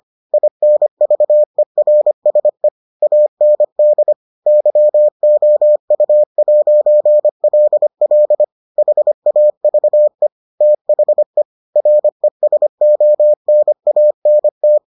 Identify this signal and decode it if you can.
Morse code, heard as a keyed tone, about 25 wpm, 600 Hz. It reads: INVERSE AND YOU'LL HAVE THE RESONANT